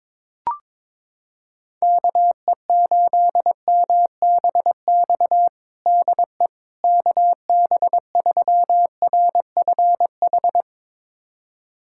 Morse code, heard as a keyed tone, about 22 wpm, 700 Hz. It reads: KE8MBX DE KB3RFH
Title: USA Callsign DE Callsign 22wpm